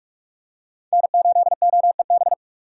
Morse code, heard as a keyed tone, about 46 wpm, 700 Hz. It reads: N8OEB
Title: USA Call Signs - 5 Characters - 4x Character Spacing - 45wpm